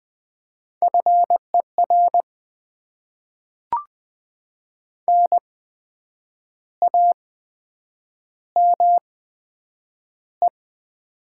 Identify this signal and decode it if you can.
Morse code, heard as a keyed tone, about 20 wpm, spaced as for 8 wpm, 700 Hz. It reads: FER NAME
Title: Top 20 QSO Elements - 8x Character Spacing - 20wpm